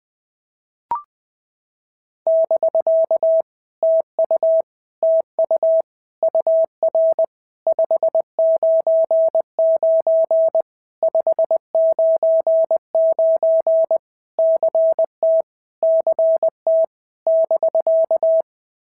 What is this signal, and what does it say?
Morse code, heard as a keyed tone, about 19 wpm, 650 Hz. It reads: <BK> TU TU UR 599 599 CT CT <BK>
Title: POTA QSO Practice - 20wpm